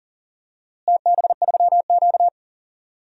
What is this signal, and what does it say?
Morse code, heard as a keyed tone, about 40 wpm, 700 Hz. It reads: TB3Q
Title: DXCC-100 Call Signs - 4 Characters - Repeated 2x - 40wpm